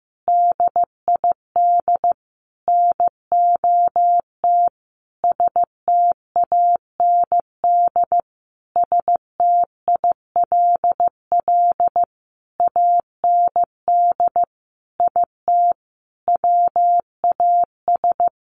Morse code, heard as a keyed tone, about 15 wpm, 700 Hz. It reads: DID NOT STAND STILL AND IT WAS
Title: Sentences from Top 800 Words 15wpm